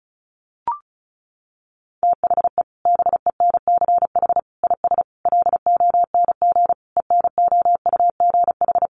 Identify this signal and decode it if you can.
Morse code, heard as a keyed tone, about 34 wpm, 700 Hz. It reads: THE BENCH IS LONG ENOUGH
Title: Sentences from Top 1400 Words 35wpm